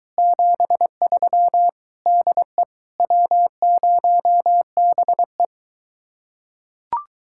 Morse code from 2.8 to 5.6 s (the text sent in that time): W0BE